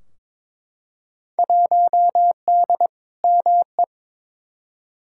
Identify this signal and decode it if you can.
Morse code, heard as a keyed tone, about 22 wpm, 700 Hz. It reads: 1D ME
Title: Field Day Exchanges - 22wpm